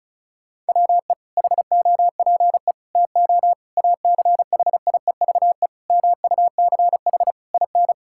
Morse code, heard as a keyed tone, about 34 wpm, 700 Hz. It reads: WE HOPE TO ACHIEVE MUCH IN
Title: Sentences from Top 1000 Words - Review Part 6 - 35wpm